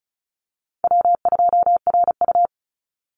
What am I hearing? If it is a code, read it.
Morse code, heard as a keyed tone, about 34 wpm, 700 Hz. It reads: W2RU
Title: Call Signs Repeated 3x - 35wpm